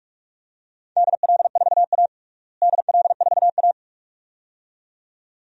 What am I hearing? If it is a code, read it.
Morse code, heard as a keyed tone, about 42 wpm, 700 Hz. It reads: DL4A DL4A
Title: DXCC-100 Call Signs - 4 Characters - Repeated 2x - 45wpm